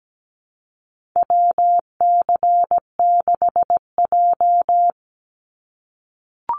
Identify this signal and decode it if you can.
Morse code, heard as a keyed tone, about 17 wpm, 700 Hz. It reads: WC6J
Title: NA SOTA Activator Callsigns 17wpm